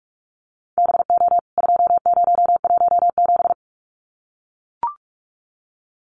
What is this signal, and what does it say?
Morse code, heard as a keyed tone, about 44 wpm, 700 Hz. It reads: BO 2017
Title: CWT Exchange 45wpm